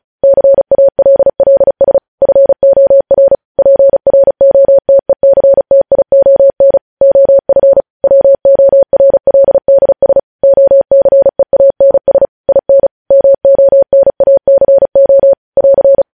Morse code, heard as a keyed tone, about 36 wpm, 550 Hz. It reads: CALLS FOR PROTECTION OF WORLDS OCEANS IN MONACO <AR>
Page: news headlines in morse code at 35 wpm